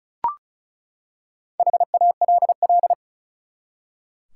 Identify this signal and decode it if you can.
Morse code, heard as a keyed tone, about 36 wpm, 700 Hz. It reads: HALL
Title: Four Letter Words - 35wpm